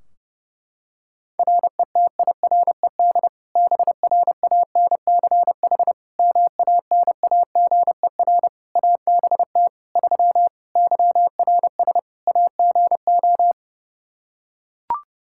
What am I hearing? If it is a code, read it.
Morse code, heard as a keyed tone, about 30 wpm, 700 Hz. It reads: RETIRED BRANCH MANAGER ABT 3 YRS AGO
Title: Practice QSO - Exchange 3 - More Info - 30wpm